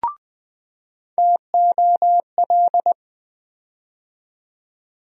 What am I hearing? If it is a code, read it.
Morse code, heard as a keyed tone, about 20 wpm, 700 Hz. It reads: TOL
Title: USA Call Sign Postfixes - 3 Letters 20wpm